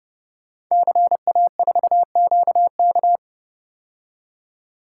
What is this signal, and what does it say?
Morse code, heard as a keyed tone, about 30 wpm, 700 Hz. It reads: CA4QK
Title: Code Groups - 5 Letter:Numbers - 30wpm